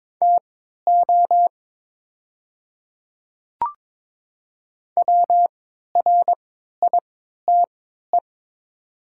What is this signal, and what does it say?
Morse code, heard as a keyed tone, about 22 wpm, spaced as for 12 wpm, 700 Hz. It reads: TO WRITE
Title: 3x Character Spacing - Top 100 Words - 22wpm